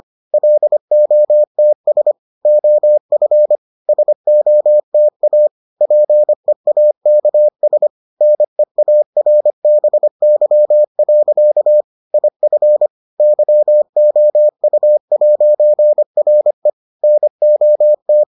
Morse code, heard as a keyed tone, about 25 wpm, 600 Hz. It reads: LOTS OF SOTA PEAKS NEARBY. IF YOU'RE NOT